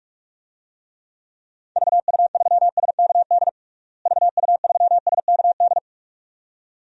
Morse code, heard as a keyed tone, about 46 wpm, 700 Hz. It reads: VU3SKD VU3SKD
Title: DXCC-100 Call Signs - 6 Characters - Repeated 2x - 45wpm